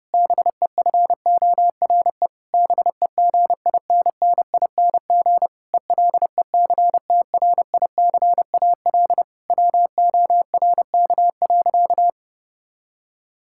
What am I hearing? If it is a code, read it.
Morse code, heard as a keyed tone, about 32 wpm, 700 Hz. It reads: BEFORE BEGINNING ELECTRICAL WORK.
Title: Sentences from Top 1000 Words - Review Part 6 - 30wpm